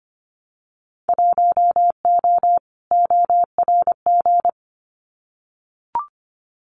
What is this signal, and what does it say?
Morse code, heard as a keyed tone, about 25 wpm, 700 Hz. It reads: 1O ORG
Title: Winter Field Day Exchanges 25wpm